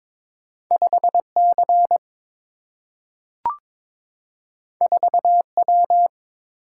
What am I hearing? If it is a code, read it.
Morse code, heard as a keyed tone, about 22 wpm, 700 Hz. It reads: 5 C 4 W
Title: Single-Letter Single-Number 22wpm